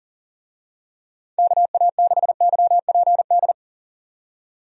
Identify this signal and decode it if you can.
Morse code, heard as a keyed tone, about 40 wpm, 700 Hz. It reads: KA6YPD
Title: USA Call Signs - 6 Characters - 40wpm